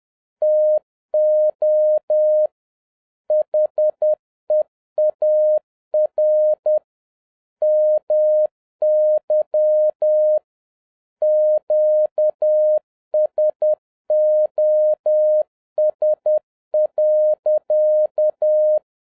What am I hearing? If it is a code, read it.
Morse code, heard as a keyed tone, about 10 wpm, 610 Hz. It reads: TO HEAR MY QSOS.